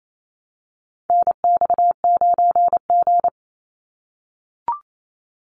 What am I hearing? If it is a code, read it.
Morse code, heard as a keyed tone, about 28 wpm, 700 Hz. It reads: NX9G
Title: Call Signs - 28wpm